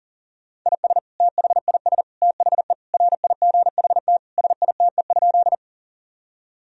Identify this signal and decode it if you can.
Morse code, heard as a keyed tone, about 40 wpm, 700 Hz. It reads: IS THIS THE RIGHT SITE?
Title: Sentences from Top 1000 Words - Review Part 4 - 40wpm